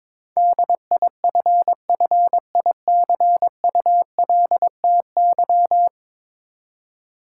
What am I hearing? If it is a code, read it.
Morse code, heard as a keyed tone, about 21 wpm, 700 Hz. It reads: DIFFICULTY
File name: practice / 22wpm / 10 Letter Words - First 3 Letters Given - 22wpm